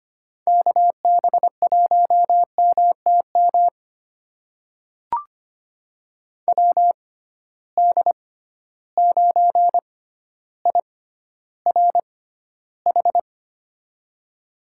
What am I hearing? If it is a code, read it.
Morse code, heard as a keyed tone, about 25 wpm, spaced as for 13 wpm, 700 Hz. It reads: KB1MTM WD9IRH
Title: USA Call Signs - 6 Characters - 6x Character Spacing - 25wpm